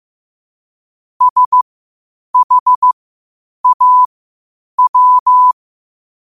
Morse code, heard as a keyed tone, about 15 wpm, spaced as for 8 wpm, 1000 Hz. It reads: SHAW
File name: qotdfiles / 8wpm